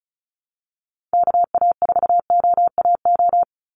Morse code, heard as a keyed tone, about 34 wpm, 700 Hz. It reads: KA4OAO
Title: USA Call Signs 35wpm